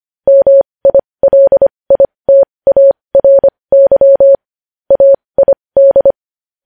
Morse code, heard as a keyed tone, about 25 wpm, spaced as for 20 wpm, 550 Hz. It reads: MILITARY AID
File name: podcast / News headlines-1x-25x20